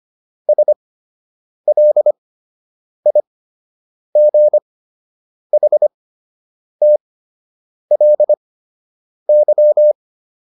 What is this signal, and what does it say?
Morse code, heard as a keyed tone, about 25 wpm, spaced as for 8 wpm, 600 Hz. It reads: SLIGHTLY